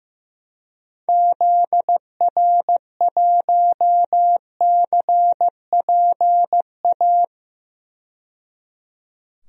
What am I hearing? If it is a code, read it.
Morse code, heard as a keyed tone, about 15 wpm, 700 Hz. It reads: ZR1CPA
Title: DXCC-100 Call Signs - 6 Characters - 15wpm